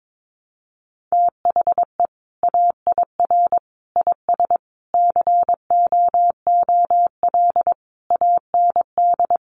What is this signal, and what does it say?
Morse code, heard as a keyed tone, about 22 wpm, 700 Hz. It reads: THE AIR IS COOL AND
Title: Sentences from Top 1200 Words 22wpm